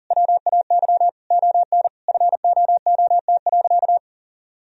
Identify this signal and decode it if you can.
Morse code, heard as a keyed tone, about 40 wpm, 700 Hz. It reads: WAY ON FOOT.
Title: Sentences from Top 1000 Words - Review Part 3 - 40wpm